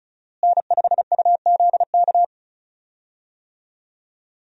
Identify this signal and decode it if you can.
Morse code, heard as a keyed tone, about 36 wpm, 700 Hz. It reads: N5UZK
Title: USA Call Signs - 5 Characters - Repeated 2x - 35wpm